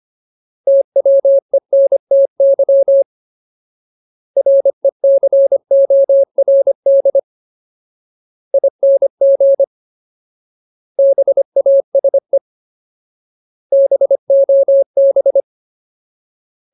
Morse code, heard as a keyed tone, about 25 wpm, spaced as for 21 wpm, 550 Hz. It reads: TWENTY RECORDING BASE BOB
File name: Ultimate Word List / 25WPM Ultimate